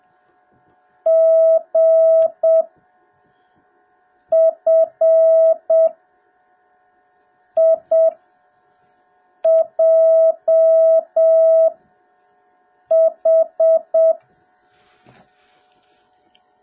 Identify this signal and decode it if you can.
Morse code, heard as a keyed tone, about 7 wpm, spaced as for 4 wpm, 650 Hz. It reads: GFIJH